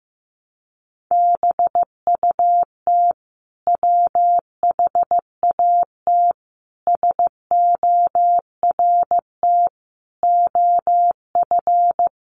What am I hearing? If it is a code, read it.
Morse code, heard as a keyed tone, about 15 wpm, 700 Hz. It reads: BUT WHAT SORT OF
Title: Sentences from Top 1300 Words 15wpm